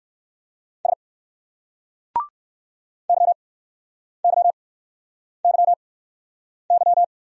Morse code, heard as a keyed tone, about 58 wpm, spaced as for 9 wpm, 700 Hz. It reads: S YYYY